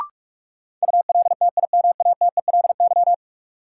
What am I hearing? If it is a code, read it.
Morse code, heard as a keyed tone, about 41 wpm, 700 Hz. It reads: ULTIMATELY
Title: Sentences from Top 1200 Words - New Word Drill Alt - 45wpm